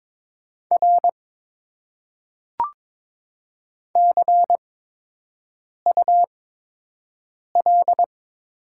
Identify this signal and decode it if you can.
Morse code, heard as a keyed tone, about 22 wpm, spaced as for 6 wpm, 700 Hz. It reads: R CUL
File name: Top 90 QSO Elements - 8x Character Spacing - 22wpm